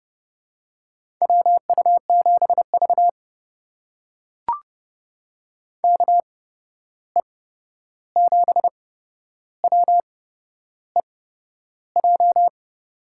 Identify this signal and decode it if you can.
Morse code, heard as a keyed tone, about 30 wpm, spaced as for 11 wpm, 700 Hz. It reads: WU7V KE7WEJ
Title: USA Call Signs - 8x Character Spacing - 30wpm